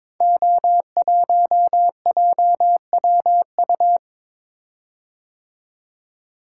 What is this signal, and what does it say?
Morse code, heard as a keyed tone, about 22 wpm, 700 Hz. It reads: O1JWU
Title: Code Groups - 5 Letter:Numbers - 22wpm